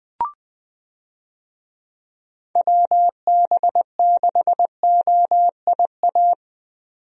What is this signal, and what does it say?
Morse code, heard as a keyed tone, about 20 wpm, 700 Hz. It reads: WB6OIA
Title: USA Call Signs - 6 Characters - 20wpm